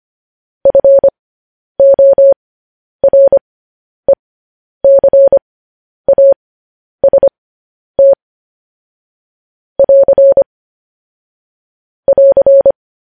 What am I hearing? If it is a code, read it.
Morse code, heard as a keyed tone, about 25 wpm, spaced as for 10 wpm, 550 Hz. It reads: FORECAST <AR> <AR>
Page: news headlines in morse code at 10 wpm